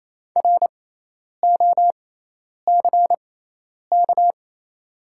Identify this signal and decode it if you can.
Morse code, heard as a keyed tone, about 28 wpm, spaced as for 10 wpm, 700 Hz. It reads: ROCK
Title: 6x Character Spacing - Top 500 Words - 28wpm